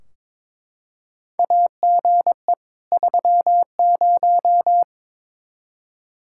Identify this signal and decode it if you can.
Morse code, heard as a keyed tone, about 22 wpm, 700 Hz. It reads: AGE 30
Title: Practice QSO - Exchange 3 - More Info - 22wpm